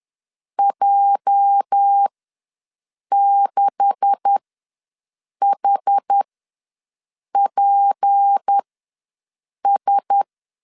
Morse code, heard as a keyed tone, about 10 wpm, spaced as for 6 wpm, 800 Hz. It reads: J6HPS
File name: LEZIONI / 10 SIGNS / 10B INT ws 10wpm 4134